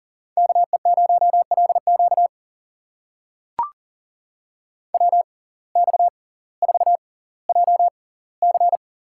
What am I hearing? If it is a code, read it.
Morse code, heard as a keyed tone, about 40 wpm, spaced as for 20 wpm, 700 Hz. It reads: KE0LQ WX4JC
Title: USA Call Signs - 6x Character Spacing - 40wpm